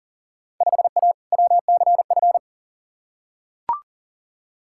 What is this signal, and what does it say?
Morse code, heard as a keyed tone, about 40 wpm, 700 Hz. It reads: 5A WCF
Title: Field Day Exchanges Encore - 40wpm